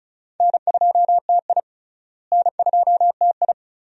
Morse code, heard as a keyed tone, about 36 wpm, 700 Hz. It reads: N2TI N2TI
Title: USA Call Signs - 4 Characters - Repeated 2x - 35wpm